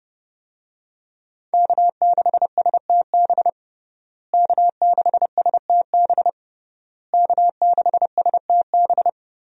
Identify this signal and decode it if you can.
Morse code, heard as a keyed tone, about 30 wpm, 700 Hz. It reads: K6STB K6STB K6STB
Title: USA Call Signs - 5 Characters - Repeated 3x - 30wpm